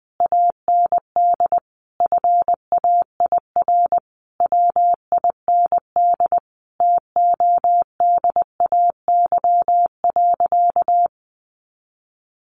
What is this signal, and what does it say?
Morse code, heard as a keyed tone, about 20 wpm, 700 Hz. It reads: AND FAIR WIND TODAY.
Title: Sentences from Top 1400 Words 20wpm